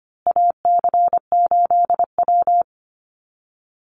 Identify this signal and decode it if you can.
Morse code, heard as a keyed tone, about 25 wpm, 700 Hz. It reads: AC8W
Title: Call Signs Repeated 2x - 25wpm